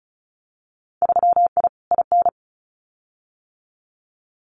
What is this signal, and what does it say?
Morse code, heard as a keyed tone, about 34 wpm, 700 Hz. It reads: 3I IN
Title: Winter Field Day Exchanges 35wpm